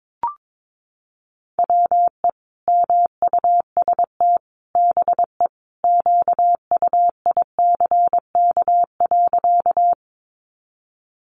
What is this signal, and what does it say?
Morse code, heard as a keyed tone, about 22 wpm, 700 Hz. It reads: WE MUST BE QUICK.
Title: Sentences from Top 500 Words 22wpm